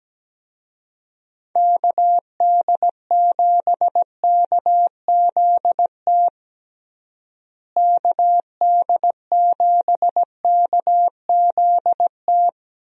Morse code, heard as a keyed tone, about 17 wpm, 700 Hz. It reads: KD7KZT KD7KZT
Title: USA Call Signs - 6 Characters - Repeated 2x - 17wpm